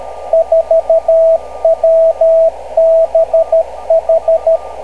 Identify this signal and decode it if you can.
Morse code, heard as a keyed tone, about 13 wpm, 650 Hz, with radio noise. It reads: 4WBH
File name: kl7fdq / beacons / kg4wbh-rst579